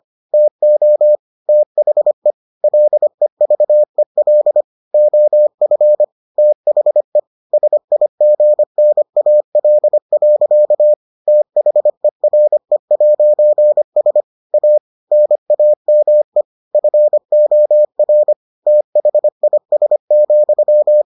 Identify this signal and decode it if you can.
Morse code, heard as a keyed tone, about 25 wpm, 600 Hz. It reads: TO THE LEVEL OF THE SIGNAL. THERE'S A NAME FOR THIS,